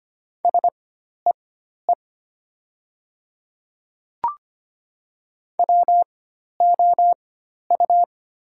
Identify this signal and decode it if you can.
Morse code, heard as a keyed tone, about 25 wpm, spaced as for 12 wpm, 700 Hz. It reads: SEE WOU
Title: 4x Character Spacing - Top 200 Words - 25wpm